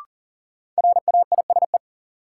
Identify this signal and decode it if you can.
Morse code, heard as a keyed tone, about 40 wpm, 700 Hz. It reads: RAISE
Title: Sentences from Top 1400 Words - New Word Drill Alt - 40wpm